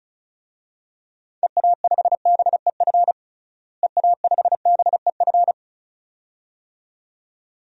Morse code, heard as a keyed tone, about 34 wpm, 700 Hz. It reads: EA5BEF EA5BEF
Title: DXCC-100 Call Signs - 6 Characters - Repeated 2x - 35wpm